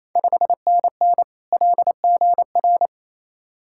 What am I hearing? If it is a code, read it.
Morse code, heard as a keyed tone, about 27 wpm, 690 Hz. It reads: 5NN LGR
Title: Ontario QSO Party Exchange 28wpm